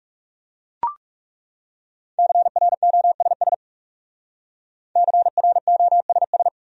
Morse code, heard as a keyed tone, about 38 wpm, 700 Hz. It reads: CROSS CROSS